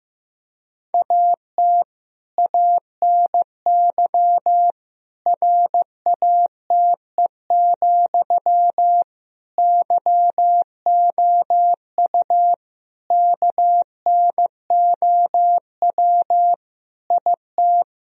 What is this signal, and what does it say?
Morse code, heard as a keyed tone, about 15 wpm, 700 Hz. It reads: AT ANY RATE, YOU KNOW IT
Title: Sentences from Top 1000 Words - Review Part 4 - 15wpm